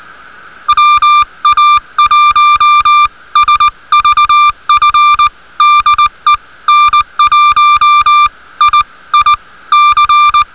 Morse code, heard as a keyed tone, about 20 wpm, spaced as for 17 wpm, 1180 Hz. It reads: WA1SVF DE N1IIC